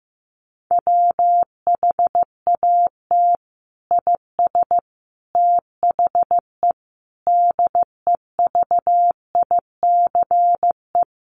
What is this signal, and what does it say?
Morse code, heard as a keyed tone, about 15 wpm, 700 Hz. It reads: WHAT IS THE DEVICE
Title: Sentences from Top 1500 Words 15wpm